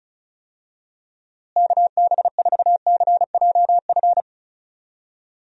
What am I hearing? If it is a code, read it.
Morse code, heard as a keyed tone, about 36 wpm, 700 Hz. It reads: KB4CJF
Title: USA Call Signs - 6 Characters - Repeated 2x - 35wpm